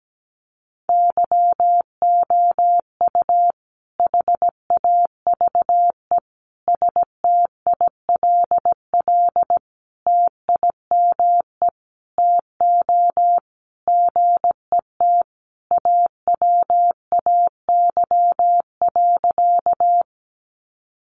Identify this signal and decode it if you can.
Morse code, heard as a keyed tone, about 17 wpm, 700 Hz. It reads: YOU HAVE STILL TIME TO GET AWAY.
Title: Sentences from Top 600 Words 17wpm